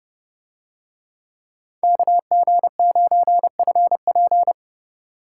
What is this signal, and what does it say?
Morse code, heard as a keyed tone, about 27 wpm, 700 Hz. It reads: KG9FP